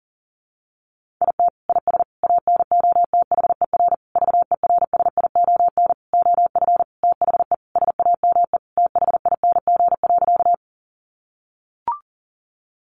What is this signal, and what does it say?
Morse code, heard as a keyed tone, about 37 wpm, 700 Hz. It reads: IT IS ANOTHER VERSION OF THE SAME THING.
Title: Sentences from Top 1400 Words 40wpm